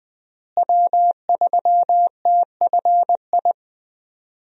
Morse code, heard as a keyed tone, about 20 wpm, 700 Hz. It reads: W3TFI
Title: USA Call Signs - 5 Characters - Repeated 3x - 20wpm